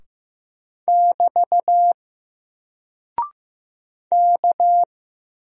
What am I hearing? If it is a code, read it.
Morse code, heard as a keyed tone, about 15 wpm, 700 Hz. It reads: <BT> K
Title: Single Character (Letter:Number:Punctuation:Prosign) 15wpm